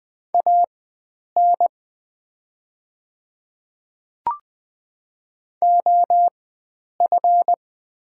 Morse code, heard as a keyed tone, about 20 wpm, spaced as for 9 wpm, 700 Hz. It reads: AN OF
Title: 4x Character Spacing - Top 300 Words - 20wpm